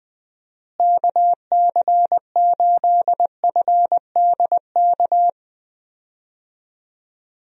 Morse code, heard as a keyed tone, about 20 wpm, 700 Hz. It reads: KC8FDK
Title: USA Call Signs - 6 Characters - Repeated 3x - 20wpm